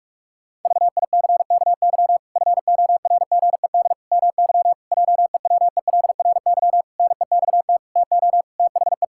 Morse code, heard as a keyed tone, about 44 wpm, 700 Hz. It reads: VICKY FORGED MY JEWELRY NEXT TO THE
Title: Pangram Challenge v2 - 45wpm